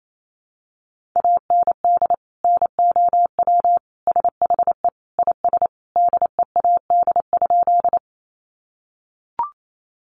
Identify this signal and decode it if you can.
Morse code, heard as a keyed tone, about 28 wpm, 700 Hz. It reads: AND NOW SHE IS DEAD?
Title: Sentences from Top 1500 Words 28wpm